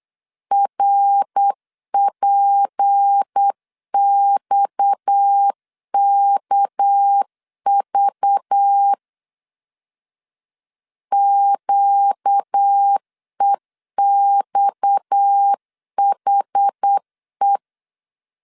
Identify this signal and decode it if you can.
Morse code, heard as a keyed tone, about 8 wpm, 800 Hz. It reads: RPXKV QEXHE
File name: LEZIONI / 6 PKX0 / 6 PKX 8wpm 3041